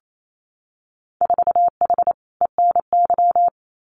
Morse code, heard as a keyed tone, about 28 wpm, 700 Hz. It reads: 4H ENY
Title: Winter Field Day Exchanges Encore 28wpm